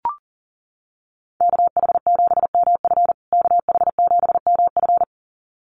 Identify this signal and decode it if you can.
Morse code, heard as a keyed tone, about 40 wpm, 700 Hz. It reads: KH7MF KH7MF
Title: USA Call Signs Repeated 2x 40wpm